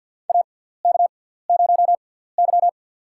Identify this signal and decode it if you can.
Morse code, heard as a keyed tone, about 52 wpm, spaced as for 17 wpm, 700 Hz. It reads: AK0Y
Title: USA Call Signs - 4 Characters - 6x Character Spacing - 50wpm